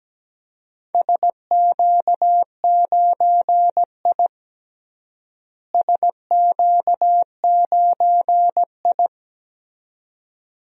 Morse code, heard as a keyed tone, about 17 wpm, 700 Hz. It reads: SQ9I SQ9I
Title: DXCC-100 Call Signs - 4 Characters - Repeated 2x - 17wpm